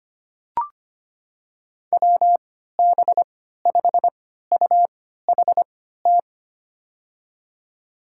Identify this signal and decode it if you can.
Morse code, heard as a keyed tone, about 25 wpm, spaced as for 14 wpm, 700 Hz. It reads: WB5UHT
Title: USA Call Signs - 3x Character Spacing - 25wpm